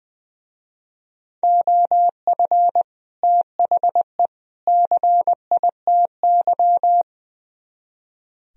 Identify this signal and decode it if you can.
Morse code, heard as a keyed tone, about 22 wpm, 700 Hz. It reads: OF THE CITY
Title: Speed-Racing - Sets of 3 Words 20wpm